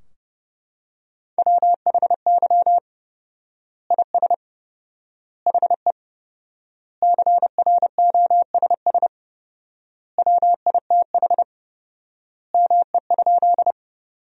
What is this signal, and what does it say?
Morse code, heard as a keyed tone, about 30 wpm, 700 Hz. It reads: WHY IS HE CROSS WITH ME?
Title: 4x Word-Spacing - Sentences from Top 300 Words - 30wpm